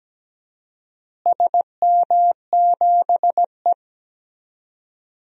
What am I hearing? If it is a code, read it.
Morse code, heard as a keyed tone, about 17 wpm, 700 Hz. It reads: SM7E
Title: DXCC-100 Call Signs - 4 Characters - Repeated 3x - 17wpm